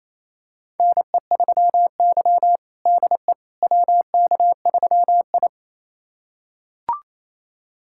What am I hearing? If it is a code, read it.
Morse code, heard as a keyed tone, about 28 wpm, 700 Hz. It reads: NE3Y DE WK3I
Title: USA Callsign DE Callsign - 4 Characters - 28wpm